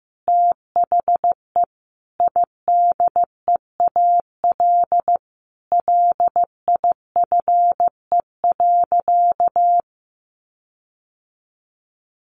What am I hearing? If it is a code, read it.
Morse code, heard as a keyed tone, about 15 wpm, 700 Hz. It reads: THE IDEAL LIFE.
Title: Sentences from Top 1400 Words 15wpm